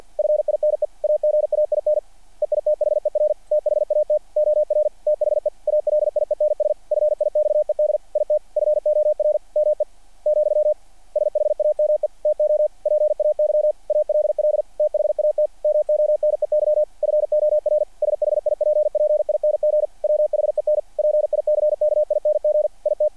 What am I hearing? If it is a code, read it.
Morse code, heard as a keyed tone, about 48 wpm, 600 Hz. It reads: FINE AGAIN EITHER THAT OR THE ALIENS FIXED IT FOR ME , SHAME TO PAY ALL THAT MONEY FOR SHIPPING WHEN PICKING IT
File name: archived / CRSnet